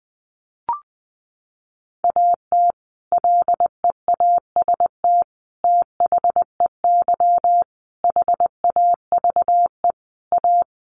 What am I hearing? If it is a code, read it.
Morse code, heard as a keyed tone, about 20 wpm, 700 Hz. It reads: AT LEAST THEY HAVE A
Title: Sentences from Top 1500 Words 20wpm